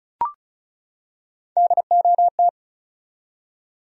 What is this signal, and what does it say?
Morse code, heard as a keyed tone, about 34 wpm, 700 Hz. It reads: DOT
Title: USA Common Names - 35wpm